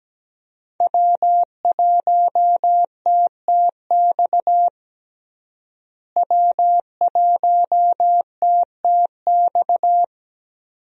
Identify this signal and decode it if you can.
Morse code, heard as a keyed tone, about 17 wpm, 700 Hz. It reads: W1TTX W1TTX
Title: USA Call Signs - 5 Characters - Repeated 2x - 17wpm